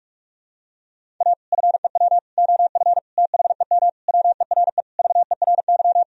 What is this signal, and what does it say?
Morse code, heard as a keyed tone, about 41 wpm, 700 Hz. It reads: A FEW OF THEM WERE VERY
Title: Sentences from Top 1000 Words - Review Part 1 - 45wpm